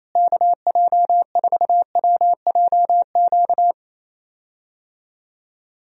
Morse code, heard as a keyed tone, about 28 wpm, 700 Hz. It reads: KJ4WJQ
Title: USA Call Signs - 6 Characters - Repeated 2x - 28wpm